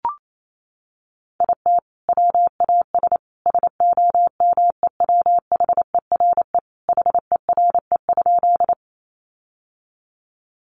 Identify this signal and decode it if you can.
Morse code, heard as a keyed tone, about 28 wpm, 700 Hz. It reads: IT WAS SOMEWHERE HERE?
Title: Sentences from Top 1400 Words 28wpm